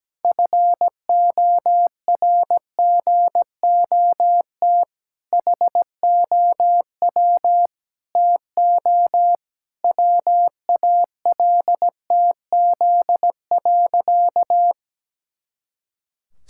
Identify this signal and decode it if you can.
Morse code, heard as a keyed tone, about 17 wpm, 700 Hz. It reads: FORGOT HOW TO WALTZ.
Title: Pangram Challenge v2 - 17wpm